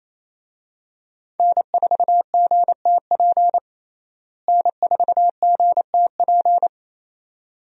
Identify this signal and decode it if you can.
Morse code, heard as a keyed tone, about 28 wpm, 700 Hz. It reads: N4GTP N4GTP
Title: USA Call Signs - 5 Characters - Repeated 2x - 28wpm